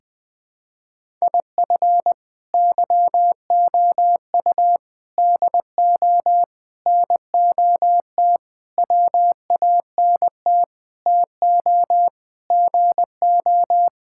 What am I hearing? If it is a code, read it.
Morse code, heard as a keyed tone, about 21 wpm, 700 Hz. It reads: IF YOU DO NOT WANT TO GO
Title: Sentences with 5-Word Phrases - 20wpm